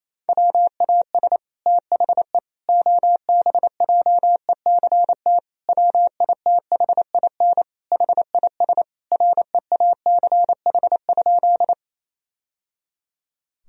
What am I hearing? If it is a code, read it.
Morse code, heard as a keyed tone, about 28 wpm, 700 Hz. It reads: WAS THE OBJECT WITHIN HIS REACH?
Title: Sentences from Top 1000 Words - Review Part 4 - 28wpm